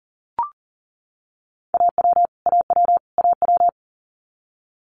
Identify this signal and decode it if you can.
Morse code, heard as a keyed tone, about 40 wpm, 700 Hz. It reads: AW AW AW
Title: Multi-letter Phonemes Repeated 3x 40wpm